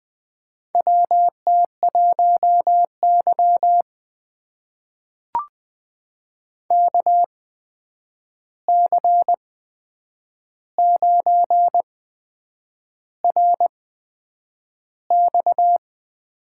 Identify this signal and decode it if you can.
Morse code, heard as a keyed tone, about 20 wpm, spaced as for 8 wpm, 700 Hz. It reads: WT1Y KC9RX
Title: USA Call Signs - 8x Character Spacing - 20wpm